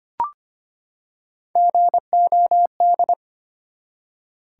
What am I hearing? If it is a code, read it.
Morse code, heard as a keyed tone, about 25 wpm, 700 Hz. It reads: GOD
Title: Three Letter Words - 25wpm